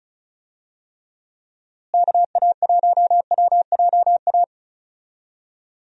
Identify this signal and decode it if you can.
Morse code, heard as a keyed tone, about 34 wpm, 700 Hz. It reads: KA1WJA
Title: USA Call Signs - 6 Characters - Repeated 3x - 35wpm